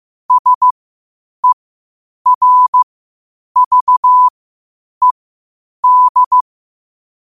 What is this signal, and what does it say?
Morse code, heard as a keyed tone, about 15 wpm, spaced as for 8 wpm, 1000 Hz. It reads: SERVED
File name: qotdfiles / 8wpm